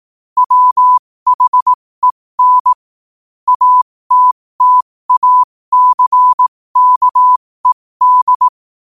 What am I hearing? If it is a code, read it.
Morse code, heard as a keyed tone, about 18 wpm, spaced as for 15 wpm, 1000 Hz. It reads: WHEN ATTACKED
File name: qotdfiles / 15wpm